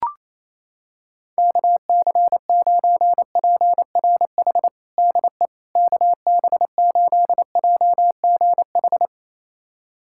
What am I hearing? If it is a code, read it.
Morse code, heard as a keyed tone, about 27 wpm, 700 Hz. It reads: KC9PRH DE KB8JGH
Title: USA Callsign DE Callsign - 6 Characters - 28wpm